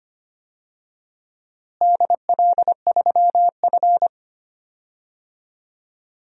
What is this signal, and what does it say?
Morse code, heard as a keyed tone, about 25 wpm, 700 Hz. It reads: DL3F
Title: DXCC-100 Call Signs - 4 Characters - Repeated 2x - 25wpm